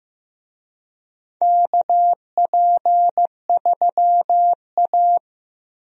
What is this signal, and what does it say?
Morse code, heard as a keyed tone, about 16 wpm, 700 Hz. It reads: KP3A
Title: USA Call Signs - 4 Characters - Repeated 2x - 15wpm